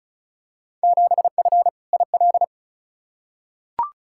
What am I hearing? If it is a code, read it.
Morse code, heard as a keyed tone, about 36 wpm, 700 Hz. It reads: 7F IL
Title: Field Day Exchanges Encore - 35wpm